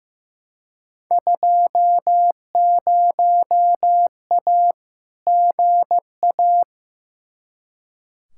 Morse code, heard as a keyed tone, about 15 wpm, 700 Hz. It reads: 20A GA
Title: Field Day Exchanges Encore - 15wpm